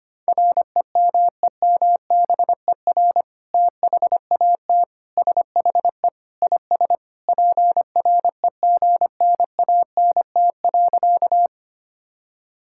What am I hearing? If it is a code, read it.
Morse code, heard as a keyed tone, about 25 wpm, 700 Hz. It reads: REMEMBER THAT SHE IS PREGNANT.
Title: Sentences from Top 1000 Words - Review Part 6 - 25wpm